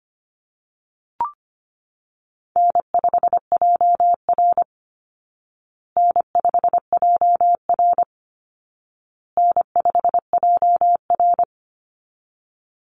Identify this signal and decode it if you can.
Morse code, heard as a keyed tone, about 25 wpm, 700 Hz. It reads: N5JR N5JR N5JR
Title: Call Signs Repeated 2x - 25wpm